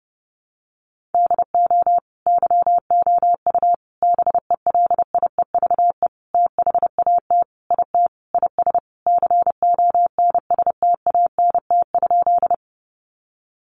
Morse code, heard as a keyed tone, about 30 wpm, 700 Hz. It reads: DO YOU BELIEVE THAT IT IS CONSTANT?
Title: Sentences from Top 1500 Words 30wpm